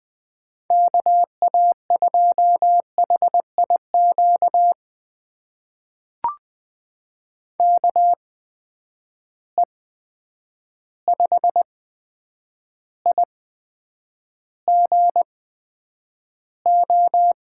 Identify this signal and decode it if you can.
Morse code, heard as a keyed tone, about 20 wpm, spaced as for 9 wpm, 700 Hz. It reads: KA2HIQ KE5IGO
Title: USA Call Signs - 6 Characters - 8x Character Spacing - 20wpm